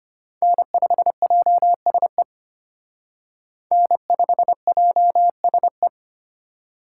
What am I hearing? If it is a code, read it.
Morse code, heard as a keyed tone, about 32 wpm, 700 Hz. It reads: N5JSE N5JSE